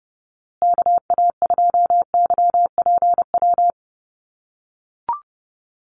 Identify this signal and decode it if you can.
Morse code, heard as a keyed tone, about 30 wpm, 700 Hz. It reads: KA2YPW
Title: USA Call Signs Repeated 3x 30wpm